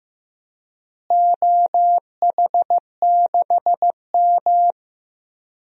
Morse code, heard as a keyed tone, about 16 wpm, 700 Hz. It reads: OH6M
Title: DXCC-100 Call Signs - 4 Characters - Repeated 2x - 15wpm